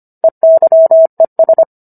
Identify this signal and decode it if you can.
Morse code, heard as a keyed tone, about 25 wpm, 650 Hz. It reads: EYES
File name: Book One / 16-Chapter 16 - 25 WPM